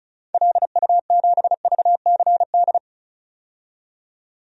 Morse code, heard as a keyed tone, about 36 wpm, 700 Hz. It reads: LU7VCD
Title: DXCC-100 Call Signs - 6 Characters - 35wpm